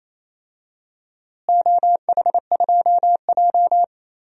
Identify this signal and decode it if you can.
Morse code, heard as a keyed tone, about 28 wpm, 700 Hz. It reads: OH2J
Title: DXCC-100 Call Signs - 4 Characters - Repeated 2x - 28wpm